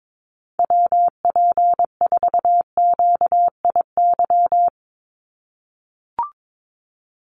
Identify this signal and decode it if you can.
Morse code, heard as a keyed tone, about 22 wpm, 700 Hz. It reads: WP4QIY
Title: USA Call Signs 22wpm